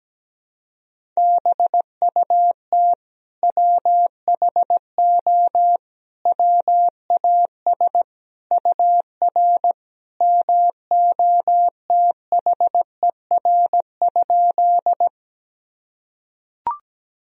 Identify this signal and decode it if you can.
Morse code, heard as a keyed tone, about 17 wpm, 700 Hz. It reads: BUT WHO WAS UR MOTHER?
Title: Sentences from Top 200 Words and Abbreviations 17wpm